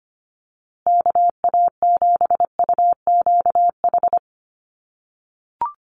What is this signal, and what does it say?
Morse code, heard as a keyed tone, about 25 wpm, 700 Hz. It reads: KA7UQH
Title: USA Call Signs Repeated 2x 25wpm